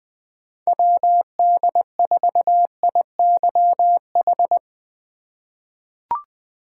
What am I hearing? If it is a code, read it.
Morse code, heard as a keyed tone, about 20 wpm, 700 Hz. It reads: WD4IYH
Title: USA Call Signs - 6 Characters - 4x Character Spacing - 20wpm